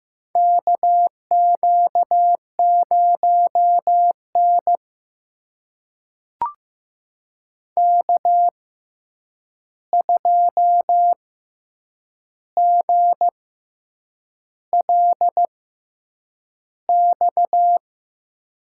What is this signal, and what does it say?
Morse code, heard as a keyed tone, about 15 wpm, spaced as for 7 wpm, 700 Hz. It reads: KQ0N K2GLX
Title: USA Call Signs - 6x Character Spacing - 15wpm